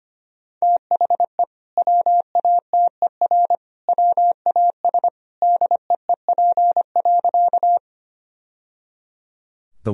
Morse code, heard as a keyed tone, about 25 wpm, 700 Hz. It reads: THE WATER WAS DEEP.
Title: Sentences from Top 1000 Words - Review Part 3 - 25wpm